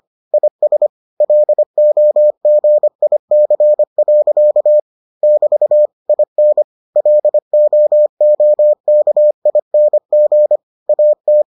Morse code, heard as a keyed tone, about 25 wpm, 600 Hz. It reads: IS LOGIC. <BT> IN LOOKING AT